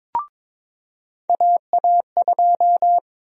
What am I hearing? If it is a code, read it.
Morse code, heard as a keyed tone, about 23 wpm, 700 Hz. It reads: AA2
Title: USA Call Sign Prefixes - 22wpm